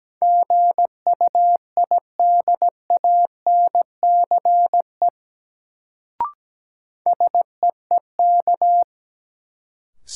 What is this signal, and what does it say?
Morse code, heard as a keyed tone, about 17 wpm, 700 Hz. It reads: GUIDANCE SEEK
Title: Sentences from Top 1600 Words - New Word Drill Alt - 17wpm